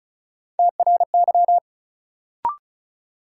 Morse code, heard as a keyed tone, about 36 wpm, 700 Hz. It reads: TRY
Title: Sets of 2 Words - New Word Drill - 35wpm